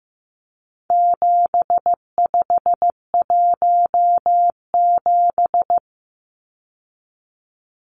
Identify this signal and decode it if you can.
Morse code, heard as a keyed tone, about 15 wpm, 700 Hz. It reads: 7517
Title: Numbers - 4-Digit 15wpm